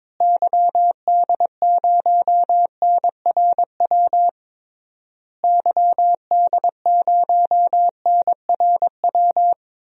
Morse code, heard as a keyed tone, about 22 wpm, 700 Hz. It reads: YD0NRW YD0NRW
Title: DXCC-100 Call Signs - 6 Characters - Repeated 3x - 22wpm